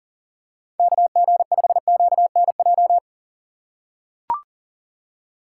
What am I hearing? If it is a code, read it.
Morse code, heard as a keyed tone, about 39 wpm, 700 Hz. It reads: KG5QNJ
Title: USA Call Signs - 6 Characters - 2x Character Spacing - 40wpm